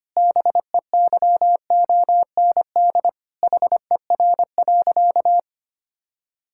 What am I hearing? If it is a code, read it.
Morse code, heard as a keyed tone, about 25 wpm, 700 Hz. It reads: BEYOND HER.
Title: Sentences from Top 1000 Words - Review Part 6 - 25wpm